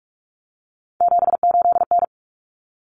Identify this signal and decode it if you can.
Morse code, heard as a keyed tone, about 44 wpm, 700 Hz. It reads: 78N
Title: Common Cut Numbers - 3-Digit 45wpm